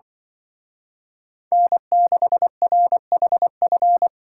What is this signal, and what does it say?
Morse code, heard as a keyed tone, about 25 wpm, 700 Hz. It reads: N6RHF
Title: Speed Racing - USA 1x3 Call Signs 20wpm